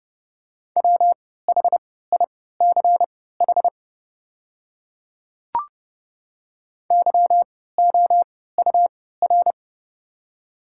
Morse code, heard as a keyed tone, about 29 wpm, spaced as for 17 wpm, 690 Hz. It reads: WHICH YOUR
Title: 3x Character Spacing - Top 200 Words - 30wpm